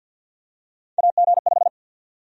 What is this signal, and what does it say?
Morse code, heard as a keyed tone, about 46 wpm, 700 Hz. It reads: AG5
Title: USA Call Sign Prefixes - 50wpm